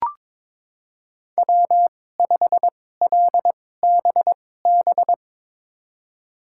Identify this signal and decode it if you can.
Morse code, heard as a keyed tone, about 22 wpm, spaced as for 16 wpm, 700 Hz. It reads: W5LBB
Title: USA Call Signs - 5 Characters - 2x Character Spacing - 22wpm